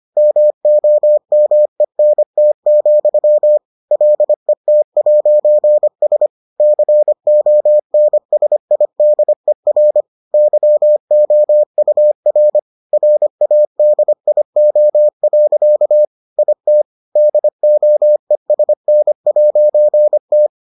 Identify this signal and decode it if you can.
Morse code, heard as a keyed tone, about 25 wpm, 600 Hz. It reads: MOMENT, LET'S CONSIDER YOUR RADIO. IT DOESN'T